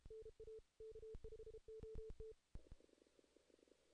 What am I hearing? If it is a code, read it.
Morse code, heard as a keyed tone, about 32 wpm, 440 Hz. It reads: NA K5OT
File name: 2016 JAN CW NAQP / 2016 Jan naqp k5ot - 0-8